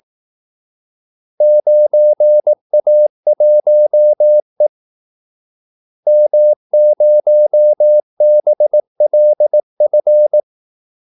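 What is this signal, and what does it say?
Morse code, heard as a keyed tone, about 18 wpm, 600 Hz. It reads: 9A1E M0BLF
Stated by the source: Morse call-sign practice at 18 wpm.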